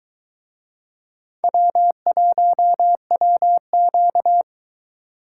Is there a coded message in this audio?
Yes: Morse code from 1.4 to 4.4 s, heard as a keyed tone, about 22 wpm, 700 Hz. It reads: W1WQ